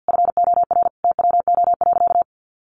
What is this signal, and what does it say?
Morse code, heard as a keyed tone, about 50 wpm, 700 Hz. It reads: FOR TWO.
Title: Sentences from Top 1400 Words 50wpm